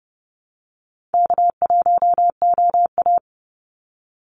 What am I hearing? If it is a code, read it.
Morse code, heard as a keyed tone, about 30 wpm, 700 Hz. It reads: K1OA
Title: Easy Call Signs Repeated 3x - 30wpm